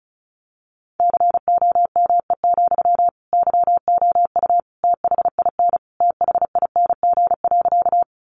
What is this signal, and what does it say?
Morse code, heard as a keyed tone, about 34 wpm, 700 Hz. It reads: COME, YOU THIN THING.
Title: Sentences from Top 1200 Words 35wpm